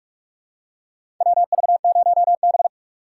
Morse code, heard as a keyed tone, about 45 wpm, 700 Hz. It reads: WV0B
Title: USA Call Signs - 4 Characters - 2x Character Spacing - 45wpm